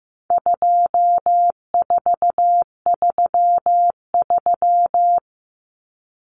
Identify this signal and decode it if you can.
Morse code, heard as a keyed tone, about 15 wpm, 700 Hz. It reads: 2433
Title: Common Cut Numbers - 4-Digit 15wpm